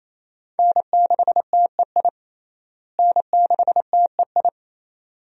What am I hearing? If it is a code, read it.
Morse code, heard as a keyed tone, about 28 wpm, 700 Hz. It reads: N6TEI N6TEI
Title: USA Call Signs - 5 Characters - Repeated 2x - 28wpm